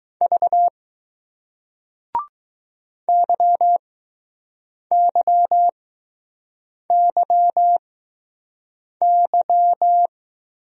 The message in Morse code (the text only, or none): V YYYY